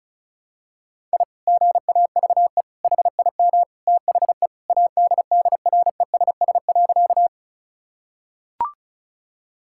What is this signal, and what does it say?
Morse code, heard as a keyed tone, about 34 wpm, 690 Hz. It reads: I GAVE HIM THE ADDRESS.
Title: Sentences from Top 1000 Words - Review Part 6 - 35wpm